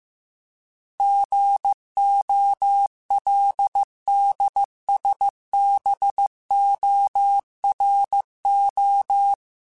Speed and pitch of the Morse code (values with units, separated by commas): 15 wpm, 800 Hz